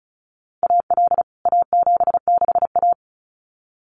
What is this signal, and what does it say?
Morse code, heard as a keyed tone, about 34 wpm, 700 Hz. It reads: AL A76A
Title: CWT Exchange 35wpm